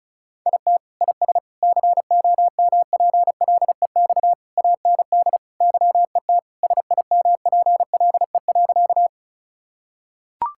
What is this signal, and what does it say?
Morse code, heard as a keyed tone, about 36 wpm, 700 Hz. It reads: IT IS COMPLEX AND YET SIMPLE.
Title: Sentences from Top 1000 Words - Review Part 6 - 35wpm